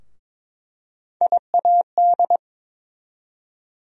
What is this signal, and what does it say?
Morse code, heard as a keyed tone, about 22 wpm, 700 Hz. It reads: IAD
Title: USA Call Sign Postfixes - 3 Letters 22wpm